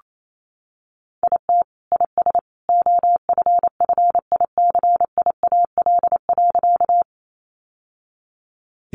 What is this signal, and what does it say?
Morse code, heard as a keyed tone, about 28 wpm, 700 Hz. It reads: IT IS OFFICIAL.
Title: Sentences from Top 1000 Words 28wpm